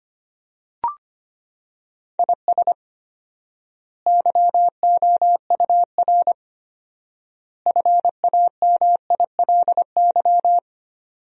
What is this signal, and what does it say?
Morse code, heard as a keyed tone, about 25 wpm, 700 Hz. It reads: IS YOUR FAMILY
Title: 4x Word-Spacing - Sentences from Top 400 Words - 25wpm